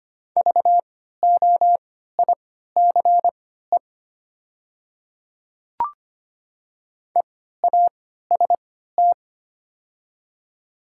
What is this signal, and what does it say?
Morse code, heard as a keyed tone, about 25 wpm, spaced as for 14 wpm, 690 Hz. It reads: VOICE EAST
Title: 3x Character Spacing - Top 500 Words - 25wpm